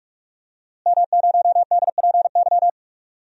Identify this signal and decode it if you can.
Morse code, heard as a keyed tone, about 43 wpm, 700 Hz. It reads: M0DPY
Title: DXCC-100 Call Signs - 5 Characters - Repeated 3x - 45wpm